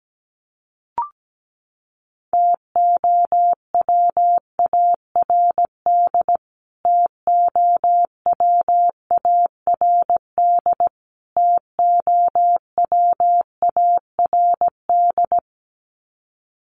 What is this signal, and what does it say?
Morse code, heard as a keyed tone, about 17 wpm, 700 Hz. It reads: TOWARD TOWARD TOWARD
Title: Top 500 Words Repeated 17wpm